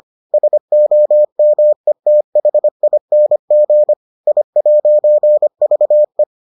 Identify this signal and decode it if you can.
Morse code, heard as a keyed tone, about 25 wpm, 600 Hz. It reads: SOMETHING I'VE